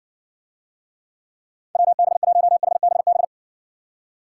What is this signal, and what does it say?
Morse code, heard as a keyed tone, about 60 wpm, 700 Hz. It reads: WB1HBB